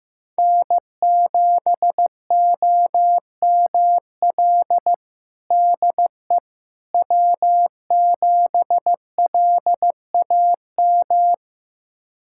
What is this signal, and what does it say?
Morse code, heard as a keyed tone, about 15 wpm, 700 Hz. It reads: N7OML DE W7LAM
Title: USA Callsign DE Callsign - 5 Characters - 15wpm